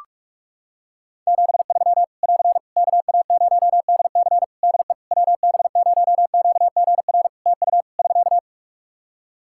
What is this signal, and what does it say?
Morse code, heard as a keyed tone, about 46 wpm, 700 Hz. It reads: 73 <AR> KA0DC DE WB0QGR TU <SK>
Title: Practice QSO - Exchange 4 - Ending - 45wpm